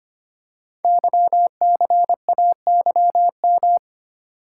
Morse code, heard as a keyed tone, about 25 wpm, 700 Hz. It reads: YCAYM
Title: Code Groups - 5 Letters - 25wpm